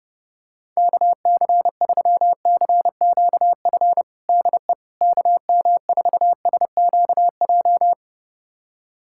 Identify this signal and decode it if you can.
Morse code, heard as a keyed tone, about 30 wpm, 700 Hz. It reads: KC3CQF DE KM4SQJ
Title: USA Callsign DE Callsign - 6 Characters - 30wpm